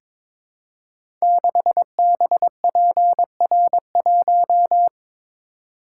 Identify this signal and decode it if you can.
Morse code, heard as a keyed tone, about 22 wpm, 700 Hz. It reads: 6BPR1
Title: Code Groups - 5 Letter:Numbers - 22wpm